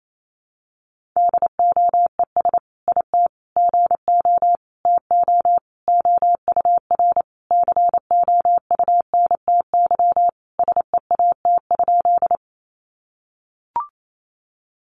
Morse code, heard as a keyed tone, about 28 wpm, 700 Hz. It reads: DOES IT GO TO OUR COUNTY SEAT?
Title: Sentences from Top 1500 Words 28wpm